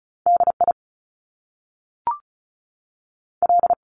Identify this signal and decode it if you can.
Morse code, heard as a keyed tone, about 36 wpm, spaced as for 6 wpm, 700 Hz. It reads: DIL
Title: Top Two Letter Combinations in Words 35wpm